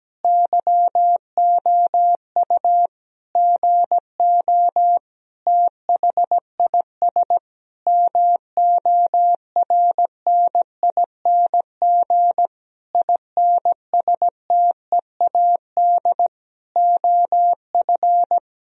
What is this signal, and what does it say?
Morse code, heard as a keyed tone, about 17 wpm, 700 Hz. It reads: YOU GO THIS MORNING INSTEAD OF
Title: Sentences from Top 1000 Words - Review Part 4 - 17wpm